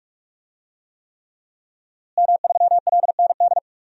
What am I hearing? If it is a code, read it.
Morse code, heard as a keyed tone, about 46 wpm, 700 Hz. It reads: M3LND
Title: DXCC-100 Call Signs - 5 Characters - Repeated 3x - 45wpm